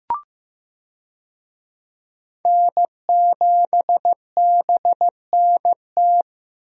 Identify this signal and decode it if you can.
Morse code, heard as a keyed tone, about 15 wpm, 700 Hz. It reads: N7BNT
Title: USA Call Signs - 5 Characters - Repeated 2x - 15wpm